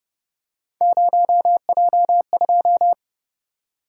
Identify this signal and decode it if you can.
Morse code, heard as a keyed tone, about 30 wpm, 700 Hz. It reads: 0J2